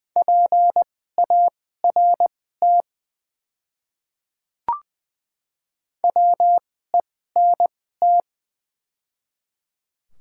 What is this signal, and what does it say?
Morse code, heard as a keyed tone, about 20 wpm, spaced as for 14 wpm, 700 Hz. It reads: PART WENT
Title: 2x Character Spacing - Top 200 Words - 20wpm